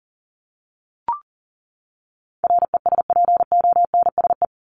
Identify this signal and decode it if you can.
Morse code, heard as a keyed tone, about 40 wpm, 700 Hz. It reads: RESPONSE
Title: Top 1251-1500 Words 40wpm